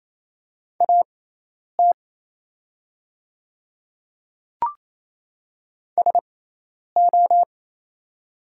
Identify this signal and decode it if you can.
Morse code, heard as a keyed tone, about 27 wpm, spaced as for 10 wpm, 690 Hz. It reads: AT SO
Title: 6x Character Spacing - Top 100 Words - 28wpm